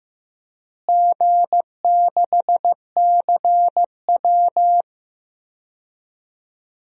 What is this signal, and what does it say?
Morse code, heard as a keyed tone, about 15 wpm, 700 Hz. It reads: G6CW
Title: DXCC-100 Call Signs - 4 Characters - Repeated 3x - 15wpm